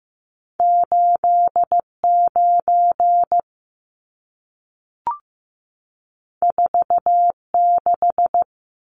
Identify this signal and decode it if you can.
Morse code, heard as a keyed tone, about 15 wpm, 700 Hz. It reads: 89 46
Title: Numbers - 2-Digit 15wpm